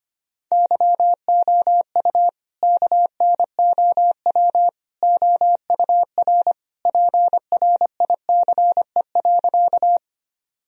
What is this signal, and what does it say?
Morse code, heard as a keyed tone, about 25 wpm, 700 Hz. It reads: YOU KNOW OUR PRICE.
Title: Sentences from Top 1000 Words - Review Part 4 - 25wpm